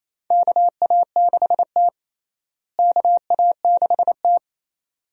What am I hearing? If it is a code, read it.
Morse code, heard as a keyed tone, about 28 wpm, 700 Hz. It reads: KA6T KA6T
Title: USA Call Signs - 4 Characters - Repeated 2x - 28wpm